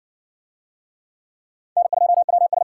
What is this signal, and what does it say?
Morse code, heard as a keyed tone, about 59 wpm, 700 Hz. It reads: N2WS